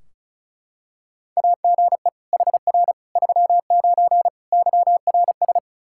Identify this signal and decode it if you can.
Morse code, heard as a keyed tone, about 36 wpm, 700 Hz. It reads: AGE HR 39 YRS
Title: Practice QSO - Exchange 3 - More Info - 35wpm